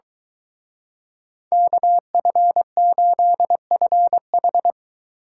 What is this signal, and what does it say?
Morse code, heard as a keyed tone, about 23 wpm, 700 Hz. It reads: KF8FH